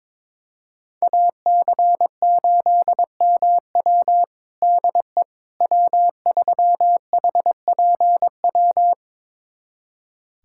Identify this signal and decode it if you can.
Morse code, heard as a keyed tone, about 22 wpm, 700 Hz. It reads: AC8MW DE W3HPW
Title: USA Callsign DE Callsign - 5 Characters - 22wpm